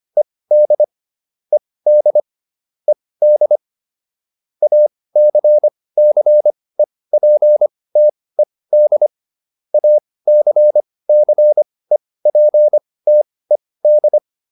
Morse code, heard as a keyed tone, about 25 wpm, spaced as for 18 wpm, 600 Hz. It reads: ED ED ED ACCEPTED ACCEPTED